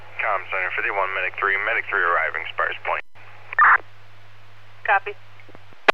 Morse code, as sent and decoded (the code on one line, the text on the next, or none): none